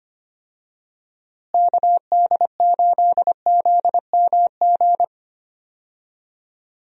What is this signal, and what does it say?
Morse code, heard as a keyed tone, about 25 wpm, 700 Hz. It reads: KD8ZMG
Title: USA Call Signs - 6 Characters - Repeated 2x - 25wpm